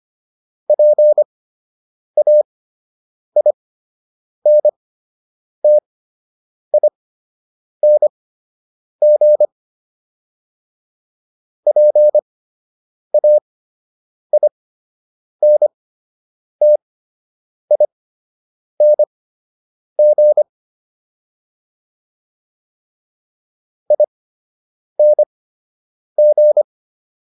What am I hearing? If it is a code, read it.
Morse code, heard as a keyed tone, about 25 wpm, spaced as for 8 wpm, 600 Hz. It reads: PAINTING PAINTING ING